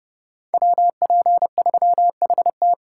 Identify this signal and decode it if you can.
Morse code, heard as a keyed tone, about 31 wpm, 700 Hz. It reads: WP3HT